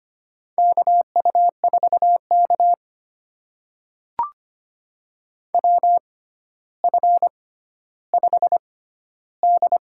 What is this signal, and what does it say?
Morse code, heard as a keyed tone, about 25 wpm, spaced as for 13 wpm, 700 Hz. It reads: KU4K WF5D
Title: USA Call Signs - 4 Characters - 6x Character Spacing - 25wpm